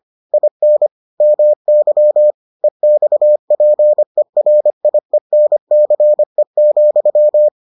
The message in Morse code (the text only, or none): IN MY EXPERIENCE,